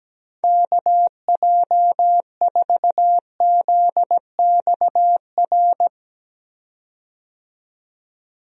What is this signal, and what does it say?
Morse code, heard as a keyed tone, about 17 wpm, 700 Hz. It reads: KJ4ZXR
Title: USA Call Signs - 6 Characters - 17wpm